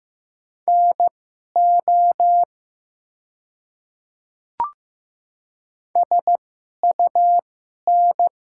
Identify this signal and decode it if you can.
Morse code, heard as a keyed tone, about 15 wpm, spaced as for 11 wpm, 700 Hz. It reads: NO SUN
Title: 2x Character Spacing - Top 300 Words - 15wpm